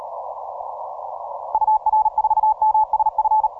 Morse code, heard as a keyed tone, about 38 wpm, 850 Hz. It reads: UR4MSF
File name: simple-file-list / TO6OK 10MCW